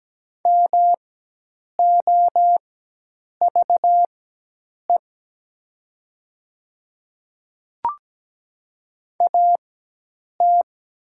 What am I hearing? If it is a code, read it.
Morse code, heard as a keyed tone, about 17 wpm, spaced as for 8 wpm, 700 Hz. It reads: MOVE AT
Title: 4x Character Spacing - Top 300 Words - 17wpm